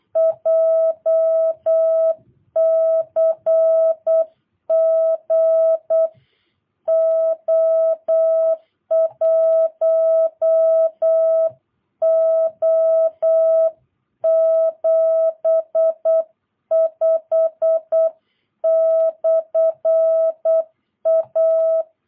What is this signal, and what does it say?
Morse code, heard as a keyed tone, about 8 wpm, 650 Hz. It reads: JCG O1O75/A
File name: jh8cbh / 2022cw